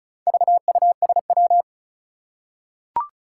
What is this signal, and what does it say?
Morse code, heard as a keyed tone, about 36 wpm, 700 Hz. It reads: VUSW
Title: Four Challenging Letter-Number Combinations 35wpm